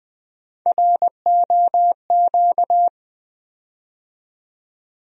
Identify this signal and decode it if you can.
Morse code, heard as a keyed tone, about 20 wpm, 700 Hz. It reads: ROQ
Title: USA Call Sign Postfixes - 3 Letters 20wpm